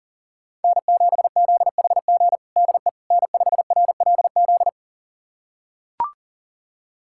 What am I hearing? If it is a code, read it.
Morse code, heard as a keyed tone, about 40 wpm, 700 Hz. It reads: N7ZHG DE N5RLZ
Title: USA Callsign DE Callsign - 5 Characters - 40wpm